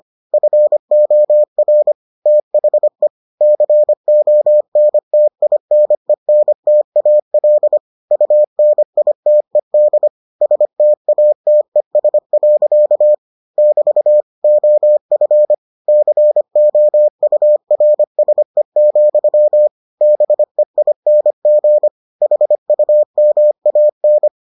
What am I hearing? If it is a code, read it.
Morse code, heard as a keyed tone, about 25 wpm, 600 Hz. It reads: FOR THE CONTINENTAL UNITED STATES. <BT> OF COURSE, BEING HUMAN